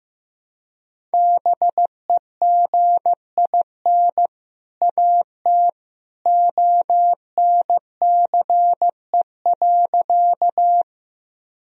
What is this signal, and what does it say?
Morse code, heard as a keyed tone, about 15 wpm, 700 Hz. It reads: BEGIN AT ONCE.
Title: Sentences from Top 1000 Words - Review Part 1 - 15wpm